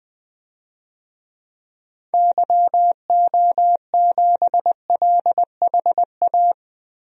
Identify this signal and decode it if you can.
Morse code, heard as a keyed tone, about 20 wpm, 700 Hz. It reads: YO7LHA
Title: DXCC-100 Call Signs - 6 Characters - Repeated 3x - 20wpm